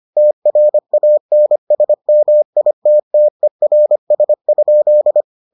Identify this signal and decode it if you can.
Morse code, heard as a keyed tone, about 25 wpm, 600 Hz. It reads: TRANSMITTERS?